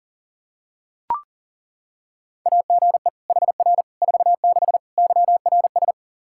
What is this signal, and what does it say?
Morse code, heard as a keyed tone, about 40 wpm, 700 Hz. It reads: AGE HR 46 YRS
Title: K1USN SST QSO Practice 40wpm